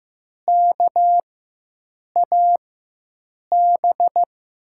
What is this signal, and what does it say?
Morse code, heard as a keyed tone, about 15 wpm, spaced as for 7 wpm, 700 Hz. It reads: KAB
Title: USA Call Signs - 6 Characters - 4x Character Spacing - 15wpm